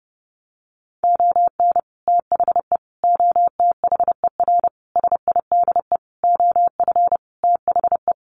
Morse code, heard as a keyed tone, about 30 wpm, 700 Hz. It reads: ON THE OTHER SIDE OF THE
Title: Sentences from Top 1100 Words 30wpm